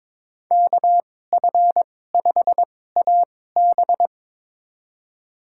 Morse code, heard as a keyed tone, about 22 wpm, spaced as for 16 wpm, 700 Hz. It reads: KF5AB
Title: USA Call Signs - 2x Character Spacing - 22wpm